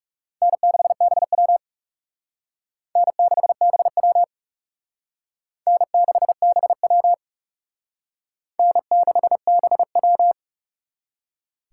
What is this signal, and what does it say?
Morse code, heard as a keyed tone, about 37 wpm, 700 Hz. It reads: N6BW N6BW N6BW N6BW
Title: Speed Racing - USA 1x2 Call Signs 30wpm